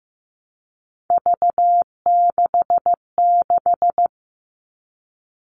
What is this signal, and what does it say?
Morse code, heard as a keyed tone, about 15 wpm, 700 Hz. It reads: V66
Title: All Cut Numbers - 3-Digit 15wpm